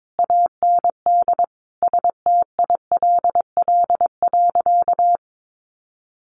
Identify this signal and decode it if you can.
Morse code, heard as a keyed tone, about 22 wpm, 700 Hz. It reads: AND STILL.
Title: Sentences from Top 500 Words 22wpm